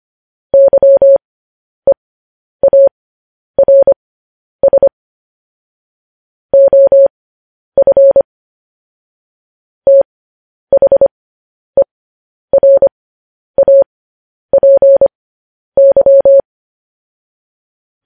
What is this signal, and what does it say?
Morse code, heard as a keyed tone, about 25 wpm, spaced as for 10 wpm, 550 Hz. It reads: YEARS OF THERAPY
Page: news-headlines-in-morse-code-at-10-wpm